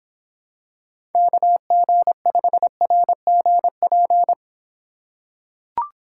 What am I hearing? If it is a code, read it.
Morse code, heard as a keyed tone, about 25 wpm, 700 Hz. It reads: KG5RGP